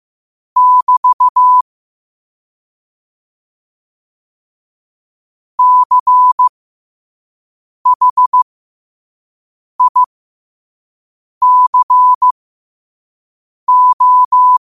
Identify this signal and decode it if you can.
Morse code, heard as a keyed tone, about 15 wpm, spaced as for 5 wpm, 1000 Hz. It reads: <BT> CHICO
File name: qotdfiles / 5wpm